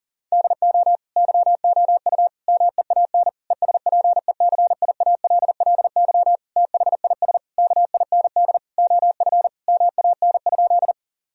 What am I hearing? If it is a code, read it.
Morse code, heard as a keyed tone, about 40 wpm, 700 Hz. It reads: DO YOU MEAN ESPECIALLY THIS KIND OF MAN?
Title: Sentences from Top 1000 Words - Review Part 4 - 40wpm